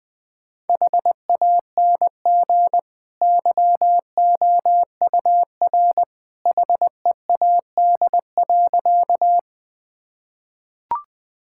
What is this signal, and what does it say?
Morse code, heard as a keyed tone, about 20 wpm, 700 Hz. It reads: HANG YOUR HEAD.
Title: Sentences from Top 1600 Words 20wpm